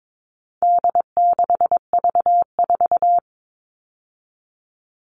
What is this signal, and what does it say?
Morse code, heard as a keyed tone, about 22 wpm, 700 Hz. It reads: D6V4
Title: All Cut Numbers - 4-Digit 22wpm